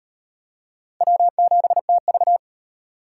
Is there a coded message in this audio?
Yes: Morse code from 1.0 to 2.4 s, heard as a keyed tone, about 34 wpm, 700 Hz. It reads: W7TV